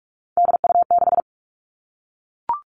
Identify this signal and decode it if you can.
Morse code, heard as a keyed tone, about 44 wpm, 700 Hz. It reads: DU6
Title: All Cut Numbers - 3-Digit 45wpm